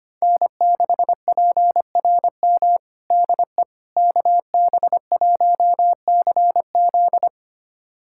Morse code, heard as a keyed tone, about 25 wpm, 700 Hz. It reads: N6PRM DE KB1CZ
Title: USA Callsign DE Callsign - 5 Characters - 25wpm